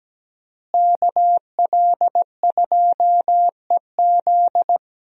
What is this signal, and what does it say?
Morse code, heard as a keyed tone, about 17 wpm, 700 Hz. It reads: KL2EZ
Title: USA Call Signs - 5 Characters - Repeated 3x - 17wpm